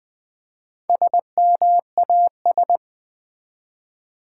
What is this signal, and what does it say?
Morse code, heard as a keyed tone, about 20 wpm, 700 Hz. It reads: SMAS
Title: Build A Word - 1 Letter At A Time - 5-Letter Words - 20wpm